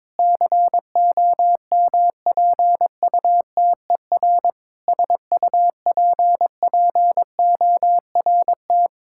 Morse code, heard as a keyed tone, about 22 wpm, 700 Hz. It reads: COMPUTER SUPPORT
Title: Practice QSO - Exchange 3 - More Info - 22wpm